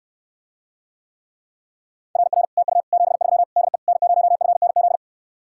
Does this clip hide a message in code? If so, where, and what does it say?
Yes: Morse code from 2.2 to 5.0 s, heard as a keyed tone, about 69 wpm, 700 Hz. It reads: RR TU 73 DE N0FNZ